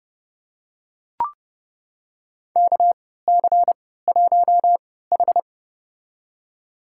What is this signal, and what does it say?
Morse code, heard as a keyed tone, about 30 wpm, spaced as for 17 wpm, 700 Hz. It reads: KC1H
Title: USA Call Signs - 3x Character Spacing - 30wpm